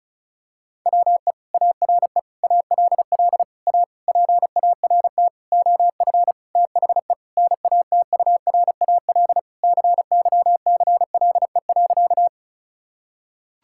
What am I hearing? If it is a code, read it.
Morse code, heard as a keyed tone, about 34 wpm, 700 Hz. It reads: WE ARE ALL A PART OF THE NATURAL CYCLE.
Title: Sentences from Top 1000 Words - Review Part 6 - 35wpm